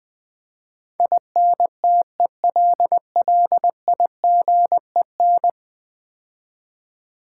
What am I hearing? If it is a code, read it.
Morse code, heard as a keyed tone, about 20 wpm, 700 Hz. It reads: INTELLIGEN
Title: Build A Word - 5 Letters At A Time - 12-Letter Words - 20wpm